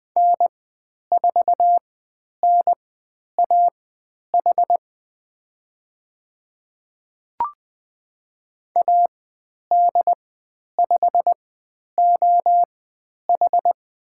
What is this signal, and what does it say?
Morse code, heard as a keyed tone, about 20 wpm, spaced as for 10 wpm, 700 Hz. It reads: N4NAH AD5OH
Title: 102 Lesson 17g - Letter u - Call Signs 20wpm